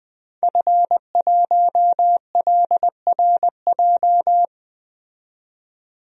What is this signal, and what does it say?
Morse code, heard as a keyed tone, about 20 wpm, 700 Hz. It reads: F1LRJ
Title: DXCC-100 Call Signs - 5 Characters - 20wpm